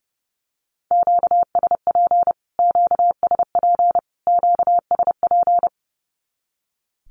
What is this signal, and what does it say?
Morse code, heard as a keyed tone, about 30 wpm, 700 Hz. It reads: QSP QSP QSP
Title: Q-Codes Repeated 3x 30wpm